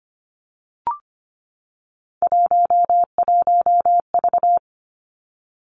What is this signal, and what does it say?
Morse code, heard as a keyed tone, about 25 wpm, 700 Hz. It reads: 11V
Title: Three Challenging Letter-Number Combinations 25wpm